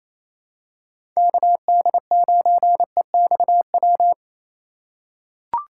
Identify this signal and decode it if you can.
Morse code, heard as a keyed tone, about 27 wpm, 700 Hz. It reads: KD9EXW
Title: USA Call Signs - 2x Character Spacing - 28wpm